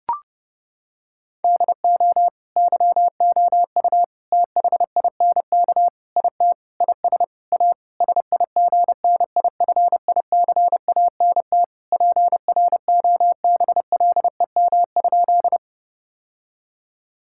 Morse code, heard as a keyed tone, about 30 wpm, 700 Hz. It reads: DO YOU THINK IT IS A SIGNIFICANT PROBLEM?
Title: Sentences from Top 1000 Words - Review Part 6 - 30wpm